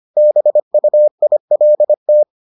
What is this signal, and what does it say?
Morse code, heard as a keyed tone, about 25 wpm, 600 Hz. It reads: BUILT